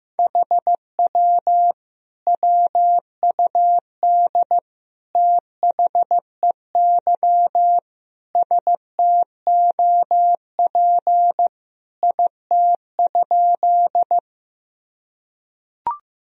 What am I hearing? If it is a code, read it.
Morse code, heard as a keyed tone, about 16 wpm, 700 Hz. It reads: HW WUD THEY STOP IT?
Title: Sentences from Top 300 Words and Abbreviations 15wpm